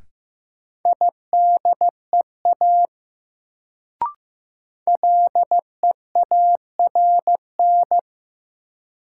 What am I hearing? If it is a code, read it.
Morse code, heard as a keyed tone, about 15 wpm, 700 Hz. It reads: IDEA LEARN
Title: Sentences from Top 300 Words - New Word Drill - 15wpm